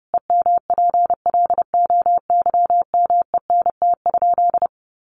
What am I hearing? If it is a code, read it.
Morse code, heard as a keyed tone, about 30 wpm, 700 Hz. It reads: EMPLOYMENT?
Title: Sentences from Top 1500 Words 30wpm